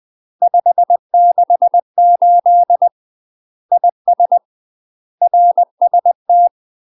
Morse code, heard as a keyed tone, about 20 wpm, 700 Hz. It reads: 568 IS RST
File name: UK MASTER QSOs TRAINING / 20WPM / UK MASTER QSOs 0024